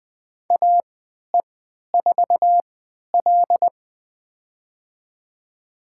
Morse code, heard as a keyed tone, about 20 wpm, spaced as for 11 wpm, 700 Hz. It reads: AE4L
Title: USA Call Signs - 3x Character Spacing - 20wpm